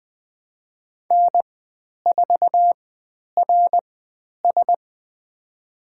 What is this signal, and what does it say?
Morse code, heard as a keyed tone, about 20 wpm, spaced as for 10 wpm, 700 Hz. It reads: N4RS
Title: 102 Lesson 17g - Letter u - Call Signs 20wpm